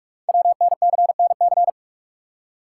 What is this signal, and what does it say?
Morse code, heard as a keyed tone, about 44 wpm, 700 Hz. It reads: WNCNC
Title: Code Groups - 5 Letters - 45wpm